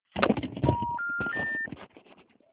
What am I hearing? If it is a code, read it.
Special information tone, with noise.
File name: out- -2006- - - .35670